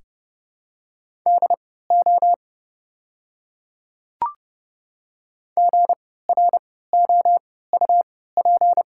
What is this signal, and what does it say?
Morse code, heard as a keyed tone, about 31 wpm, spaced as for 17 wpm, 700 Hz. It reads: DO GROUP
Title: 3x Character Spacing - Top 500 Words - 30wpm